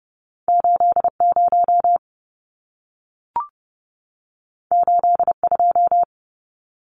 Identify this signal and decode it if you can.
Morse code, heard as a keyed tone, about 30 wpm, 700 Hz. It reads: 80 82
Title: Numbers - 2-Digit 30wpm